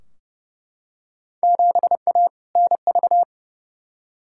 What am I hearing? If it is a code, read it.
Morse code, heard as a keyed tone, about 30 wpm, 700 Hz. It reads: 7A NV
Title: Field Day Exchanges Encore - 30wpm